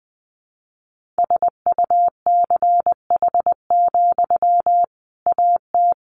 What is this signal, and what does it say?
Morse code, heard as a keyed tone, about 20 wpm, 700 Hz. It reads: SUCH, AT
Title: Sentences from Top 1500 Words 20wpm